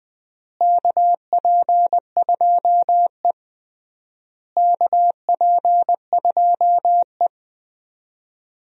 Morse code, heard as a keyed tone, about 20 wpm, 700 Hz. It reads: KP2E KP2E
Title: DXCC-100 Call Signs - 4 Characters - Repeated 3x - 20wpm